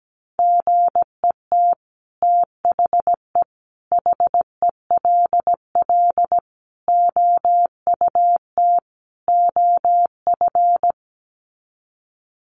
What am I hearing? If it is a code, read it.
Morse code, heard as a keyed tone, about 17 wpm, 700 Hz. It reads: GET THE HELL OUT OF
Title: Sets of 5 Words Encore2 17wpm